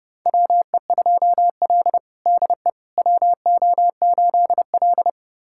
Morse code, heard as a keyed tone, about 30 wpm, 700 Hz. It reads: WE2L DE WO8L
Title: USA Callsign DE Callsign - 4 Characters - 30wpm